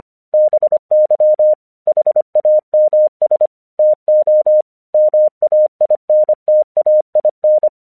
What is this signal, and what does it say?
Morse code, heard as a keyed tone, about 25 wpm, 610 Hz. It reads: BY HAMS TO MAINTAIN
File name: 1 English 25wpm / AQC05